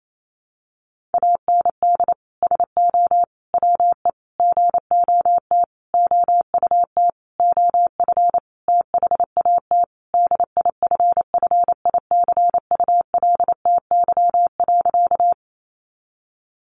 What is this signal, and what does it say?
Morse code, heard as a keyed tone, about 28 wpm, 700 Hz. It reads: AND SO WE GOT OUT OF THAT DIFFICULTY.
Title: Sentences from Top 1500 Words 28wpm